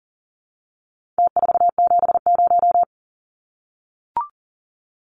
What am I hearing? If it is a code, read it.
Morse code, heard as a keyed tone, about 40 wpm, 700 Hz. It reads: T470
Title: Numbers - Contest Style - 4-Digit 40wpm